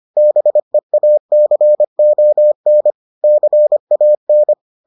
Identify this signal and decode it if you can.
Morse code, heard as a keyed tone, about 25 wpm, 600 Hz. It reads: BEACON CAN